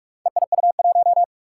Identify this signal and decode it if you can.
Morse code, heard as a keyed tone, about 44 wpm, 700 Hz. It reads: EIU1
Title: Four Challenging Letter-Number Combinations 45wpm